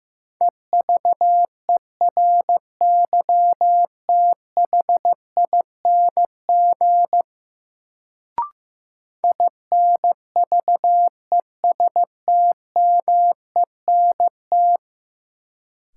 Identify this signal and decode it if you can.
Morse code, heard as a keyed tone, about 15 wpm, 700 Hz. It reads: EVERYTHING INVESTMENT
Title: Sentences from Top 700 Words - New Word Drill - 15wpm